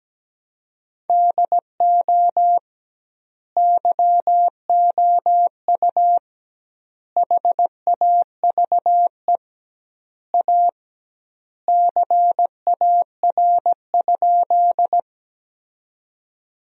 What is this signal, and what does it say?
Morse code, heard as a keyed tone, about 17 wpm, 700 Hz. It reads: DO YOU HAVE A CAR?
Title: 2x Word-Spacing - Sentences from Top 300 Words - 17wpm